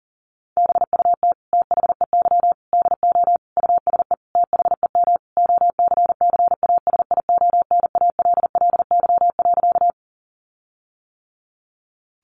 Morse code, heard as a keyed tone, about 40 wpm, 700 Hz. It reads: BUT THEY DO USE THEM OCCASIONALLY.
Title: Sentences from Top 1300 Words 40wpm